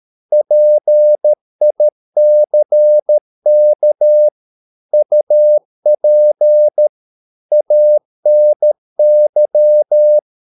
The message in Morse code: PICK UP ANY

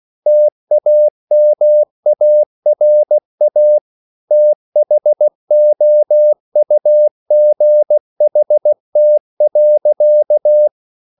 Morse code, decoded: TAMARA THOUGHT.

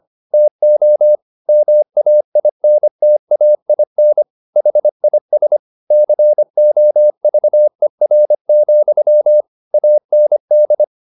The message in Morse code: TO MAINTAIN HIS COVER, AND

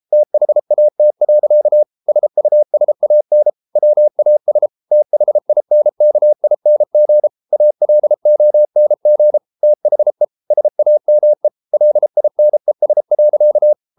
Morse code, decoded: THAT. SUSAN WAS THINKING ALONG THE SAME LINES.

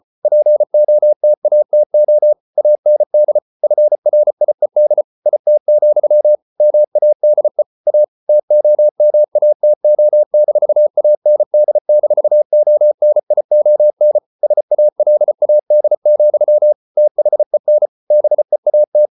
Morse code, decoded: POTATO AND FRIED IT, MADE A TOMATO-AND-ONION SALAD, THEN BEAT

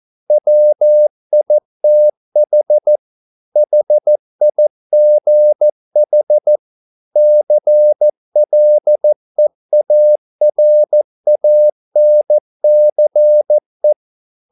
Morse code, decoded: WITH HIGH CLEARANCE

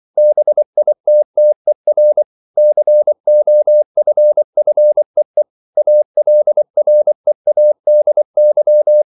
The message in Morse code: BITTER COFFEE ALREADY

